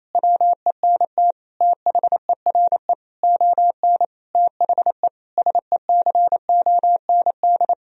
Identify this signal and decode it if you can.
Morse code, heard as a keyed tone, about 28 wpm, 700 Hz. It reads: WENT THERE ON THE SECOND
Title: Sentences from Top 1000 Words - Review Part 4 - 28wpm